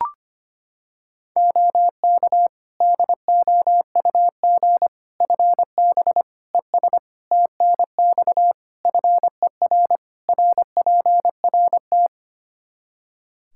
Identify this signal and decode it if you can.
Morse code, heard as a keyed tone, about 25 wpm, 700 Hz. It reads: OK DOUG FB ES TNX FER RPRT
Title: Practice QSO - Exchange 2 - Info - 25wpm